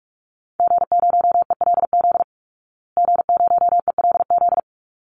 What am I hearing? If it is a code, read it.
Morse code, heard as a keyed tone, about 44 wpm, 700 Hz. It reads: G0ELZ G0ELZ
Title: Call Signs Repeated 3x - 45wpm